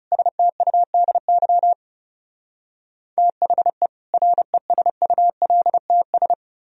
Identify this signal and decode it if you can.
Morse code, heard as a keyed tone, about 37 wpm, 700 Hz. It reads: STUDY THE RESULTS
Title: Speed-Racing - Sets of 5 Words 30wpm